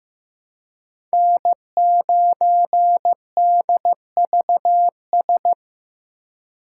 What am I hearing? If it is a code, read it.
Morse code, heard as a keyed tone, about 15 wpm, 700 Hz. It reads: N9DVS
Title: USA Call Signs - 5 Characters - 2x Character Spacing - 15wpm